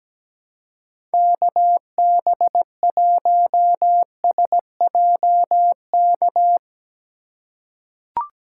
Morse code, heard as a keyed tone, about 17 wpm, 700 Hz. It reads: KB1SJK
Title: USA Call Signs - 6 Characters - 2x Character Spacing - 17wpm